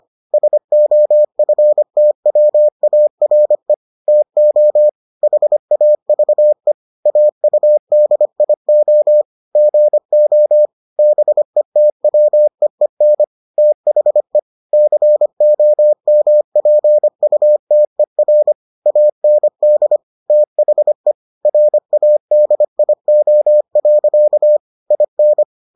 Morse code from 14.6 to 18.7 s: COMPUTER

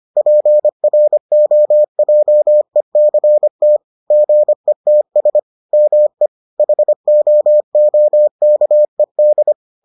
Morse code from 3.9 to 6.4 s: GETS ME

